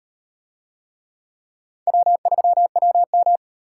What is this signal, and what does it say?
Morse code, heard as a keyed tone, about 37 wpm, 700 Hz. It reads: W3WM